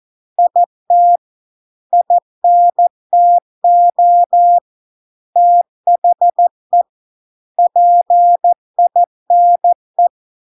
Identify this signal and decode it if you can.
Morse code, heard as a keyed tone, about 14 wpm, 700 Hz. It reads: IT INTO THE PINE